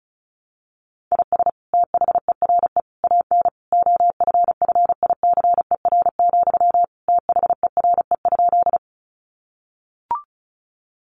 Morse code, heard as a keyed tone, about 34 wpm, 700 Hz. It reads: IS THERE AN OFFICER, THERE?
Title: Sentences from Top 1200 Words 35wpm